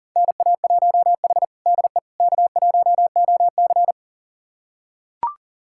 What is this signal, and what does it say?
Morse code, heard as a keyed tone, about 40 wpm, 700 Hz. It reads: NA1H DE K1OC
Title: USA Callsign DE Callsign - 4 Characters - 40wpm